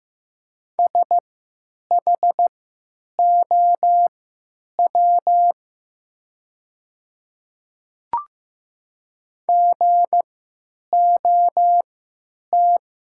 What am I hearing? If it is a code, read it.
Morse code, heard as a keyed tone, about 15 wpm, spaced as for 9 wpm, 700 Hz. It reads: SHOW GOT
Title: 3x Character Spacing - Top 500 Words - 15wpm